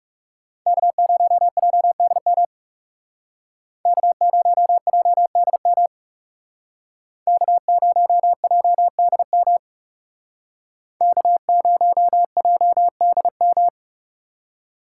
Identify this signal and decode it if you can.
Morse code, heard as a keyed tone, about 37 wpm, 700 Hz. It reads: K0JDM K0JDM K0JDM K0JDM
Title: Speed Racing - USA 1x3 Call Signs 30wpm